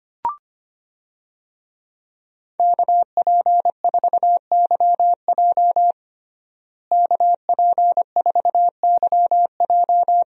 Morse code, heard as a keyed tone, about 25 wpm, 700 Hz. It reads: KP4YJ KP4YJ
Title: DXCC-100 Call Signs - 5 Characters - Repeated 2x - 25wpm